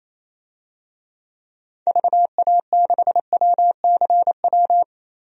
Morse code, heard as a keyed tone, about 27 wpm, 700 Hz. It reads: VA6WCW
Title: DXCC-100 Call Signs - 6 Characters - 28wpm